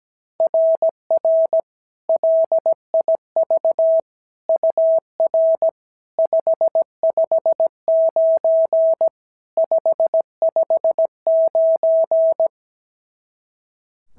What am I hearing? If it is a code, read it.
Morse code, heard as a keyed tone, about 16 wpm, 650 Hz. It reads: RR LIV UR 559 559
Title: SOTA QSO Practice 17wpm